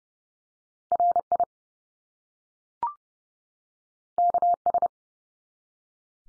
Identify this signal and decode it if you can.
Morse code, heard as a keyed tone, about 30 wpm, 700 Hz. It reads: RI KS
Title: US States 30WPM